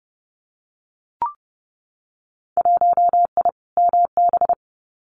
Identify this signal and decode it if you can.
Morse code, heard as a keyed tone, about 30 wpm, 700 Hz. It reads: 1I MB
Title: Winter Field Day Exchanges Encore 30wpm